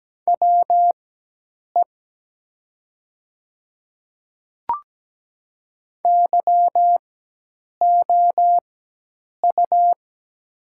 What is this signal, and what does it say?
Morse code, heard as a keyed tone, about 17 wpm, spaced as for 8 wpm, 700 Hz. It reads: WE YOU
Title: 4x Character Spacing - Top 100 Words - 17wpm